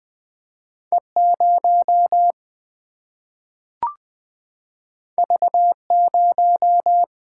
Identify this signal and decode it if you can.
Morse code, heard as a keyed tone, about 20 wpm, 700 Hz. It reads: E 0 V 0
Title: Single-Letter Single-Number 20wpm